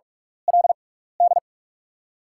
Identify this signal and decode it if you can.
Morse code, heard as a keyed tone, about 44 wpm, spaced as for 15 wpm, 690 Hz. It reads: LD
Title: 6x Character Spacing - Top 300 Words - 45wpm